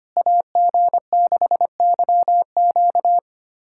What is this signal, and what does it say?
Morse code, heard as a keyed tone, about 25 wpm, 700 Hz. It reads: AG6YQ
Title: USA Callsign DE Callsign 25wpm